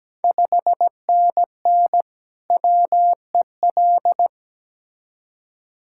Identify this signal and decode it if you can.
Morse code, heard as a keyed tone, about 17 wpm, 690 Hz. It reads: 5NN WEL
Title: Ontario QSO Party Exchange 17wpm